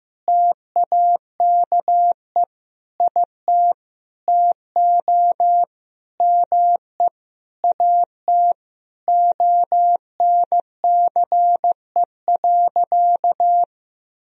Morse code, heard as a keyed tone, about 15 wpm, 700 Hz. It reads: TAKE IT TO ME AT ONCE.
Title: Sentences from Top 500 Words and Abbreviations 15wpm